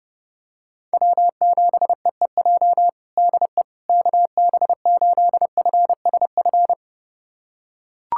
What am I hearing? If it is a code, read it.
Morse code, heard as a keyed tone, about 30 wpm, 700 Hz. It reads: W7EEJ DE KB8FSF
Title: USA Callsign DE Callsign 30wpm